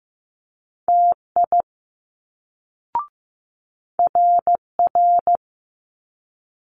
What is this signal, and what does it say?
Morse code, heard as a keyed tone, about 15 wpm, 700 Hz. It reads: TI RR
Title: Multi-letter Phonemes Quick Fire 15wpm